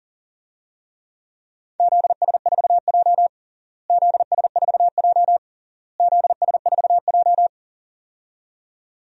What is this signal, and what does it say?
Morse code, heard as a keyed tone, about 37 wpm, 700 Hz. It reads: ZS4J ZS4J ZS4J
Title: DXCC-100 Call Signs - 4 Characters - Repeated 3x - 40wpm